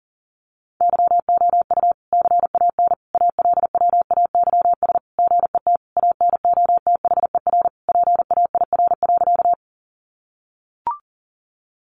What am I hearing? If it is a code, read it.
Morse code, heard as a keyed tone, about 40 wpm, 700 Hz. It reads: YOU CAN ALWAYS GET ANOTHER PAIR.
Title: Sentences from Top 1200 Words 40wpm